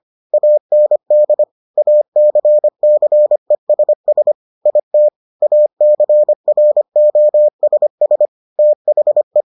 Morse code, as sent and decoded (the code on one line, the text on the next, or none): .- -. -.. / .- -.-. -.-. ... ... / .. - / .- -.-. .-. --- ... ... / - .... .
AND ACCESS IT ACROSS THE